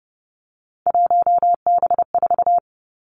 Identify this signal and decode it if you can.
Morse code, heard as a keyed tone, about 30 wpm, 700 Hz. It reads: 1B4
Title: Three Challenging Letter-Number Combinations 30wpm